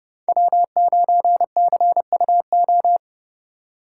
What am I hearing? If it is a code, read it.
Morse code, heard as a keyed tone, about 29 wpm, 700 Hz. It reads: W9CUO